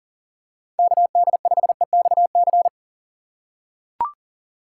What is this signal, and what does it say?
Morse code, heard as a keyed tone, about 37 wpm, 700 Hz. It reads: KD5EXC
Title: USA Call Signs - 6 Characters - 3x Character Spacing - 40wpm